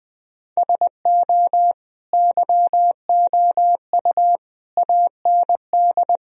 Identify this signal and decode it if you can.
Morse code, heard as a keyed tone, about 20 wpm, 700 Hz. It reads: SO YOU AND
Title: Sentences from Top 1000 Words - Review Part 4 - 20wpm